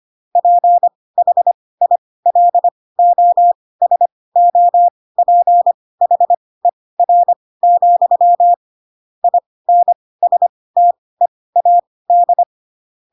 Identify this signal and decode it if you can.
Morse code, heard as a keyed tone, about 25 wpm, spaced as for 18 wpm, 700 Hz. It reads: PHILOSOPHER, INSTEAD
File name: Pinocchio / Pinocchio ch4 12 25 ik1hge